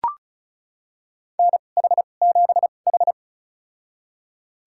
Morse code, heard as a keyed tone, about 34 wpm, spaced as for 25 wpm, 700 Hz. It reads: NH7H
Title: USA Call Signs - 2x Character Spacing - 35wpm